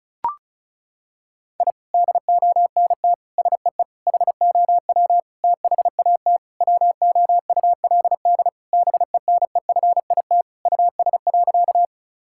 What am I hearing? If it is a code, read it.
Morse code, heard as a keyed tone, about 36 wpm, 700 Hz. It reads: I DONT SEE HOW THAT WOULD BENEFIT US.
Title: Sentences from Top 1000 Words - Review Part 6 - 35wpm